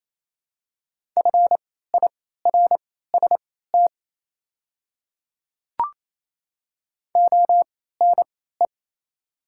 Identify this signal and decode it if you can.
Morse code, heard as a keyed tone, about 28 wpm, spaced as for 16 wpm, 690 Hz. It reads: FIRST ONE
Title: 3x Character Spacing - Top 100 Words - 28wpm